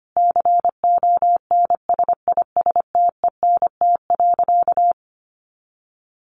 Morse code, heard as a keyed tone, about 25 wpm, 700 Hz. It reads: CONSISTENT.
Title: Sentences from Top 1500 Words 25wpm